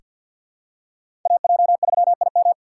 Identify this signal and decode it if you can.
Morse code, heard as a keyed tone, about 46 wpm, 700 Hz. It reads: AJ3IM
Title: USA Call Signs - 5 Characters - 6x Character Spacing - 50wpm